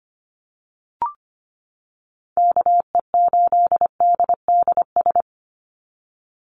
Morse code, heard as a keyed tone, about 25 wpm, 700 Hz. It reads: KE8DDS
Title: USA Call Signs 25wpm